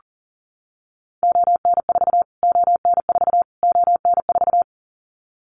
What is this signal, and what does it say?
Morse code, heard as a keyed tone, about 40 wpm, 700 Hz. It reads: ON4 ON4 ON4
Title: Callsign Prefixes for Top 25 DXCC - Repeated 3x - 40wpm